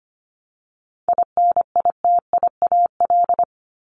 Morse code, heard as a keyed tone, about 25 wpm, 700 Hz. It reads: INITIAL
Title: Top 1501-1750 Words 25wpm